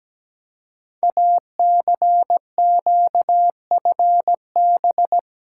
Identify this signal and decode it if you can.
Morse code, heard as a keyed tone, about 17 wpm, 700 Hz. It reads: ACQFB
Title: Code Groups - 5 Letters - 17wpm